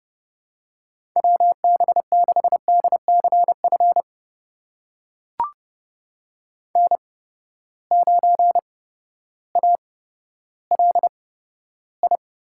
Morse code, heard as a keyed tone, about 30 wpm, spaced as for 14 wpm, 700 Hz. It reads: WB6DCF N9ALI
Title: USA Call Signs - 8x Character Spacing - 30wpm